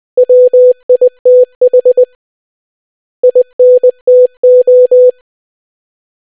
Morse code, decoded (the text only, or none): WITH INTO